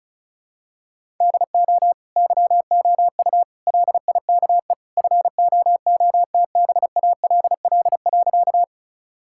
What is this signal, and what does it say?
Morse code, heard as a keyed tone, about 34 wpm, 700 Hz. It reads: DO YOU LIKE FOOTBALL.
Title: Sentences from Top 1600 Words 35wpm